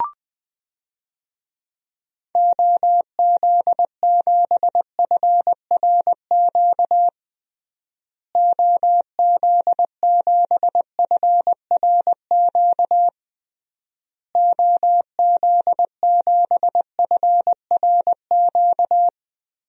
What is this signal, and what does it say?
Morse code, heard as a keyed tone, about 20 wpm, 700 Hz. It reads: OZ7FRQ OZ7FRQ OZ7FRQ
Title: DXCC-100 Call Signs - 6 Characters - Repeated 3x - 20wpm